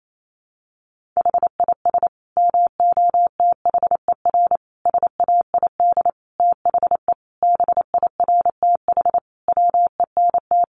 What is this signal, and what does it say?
Morse code, heard as a keyed tone, about 28 wpm, 700 Hz. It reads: HIS MOTHER SAID THE BIRTH WENT
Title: Sentences from Top 800 Words 28wpm